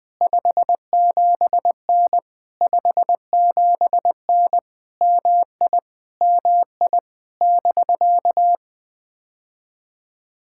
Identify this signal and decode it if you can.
Morse code, heard as a keyed tone, about 20 wpm, 700 Hz. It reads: 57N 57N MI MI <BK>
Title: POTA QSO Practice - 20wpm